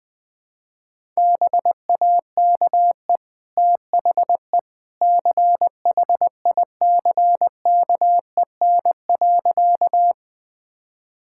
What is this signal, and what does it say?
Morse code, heard as a keyed tone, about 20 wpm, 700 Hz. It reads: BAKE THE CHICKEN.
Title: Easy Sentences with Hints - 20wpm